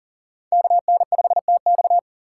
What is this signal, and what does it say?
Morse code, heard as a keyed tone, about 40 wpm, 700 Hz. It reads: KN5TX
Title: K1USN SST QSO Practice - 40wpm